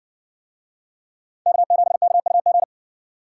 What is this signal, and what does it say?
Morse code, heard as a keyed tone, about 58 wpm, 700 Hz. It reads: K7KUG